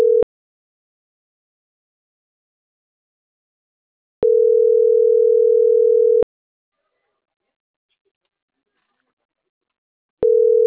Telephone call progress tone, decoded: ringback tone

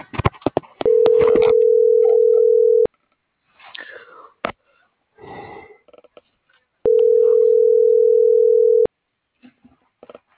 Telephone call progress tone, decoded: ringback tone

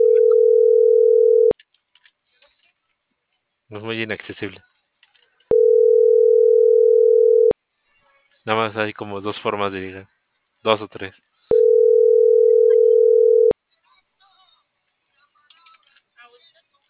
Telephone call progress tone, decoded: ringback tone

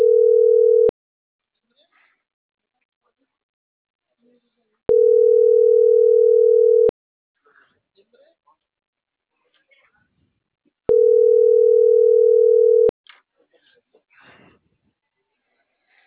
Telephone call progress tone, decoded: ringback tone

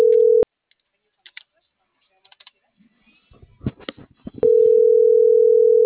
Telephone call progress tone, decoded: ringback tone

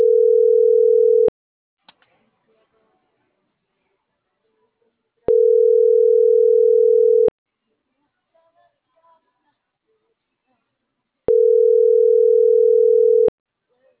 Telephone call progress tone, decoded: ringback tone

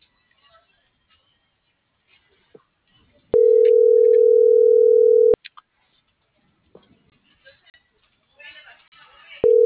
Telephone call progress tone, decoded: ringback tone

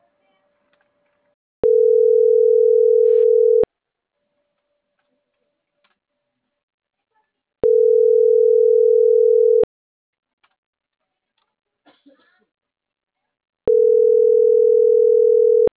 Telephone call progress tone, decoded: ringback tone

